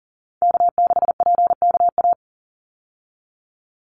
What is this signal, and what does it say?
Morse code, heard as a keyed tone, about 40 wpm, 700 Hz. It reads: K6PKA
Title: USA Call Signs 40wpm